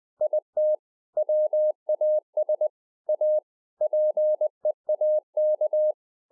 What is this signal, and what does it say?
Morse code, heard as a keyed tone, about 20 wpm, 610 Hz. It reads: IT WAS A PEAK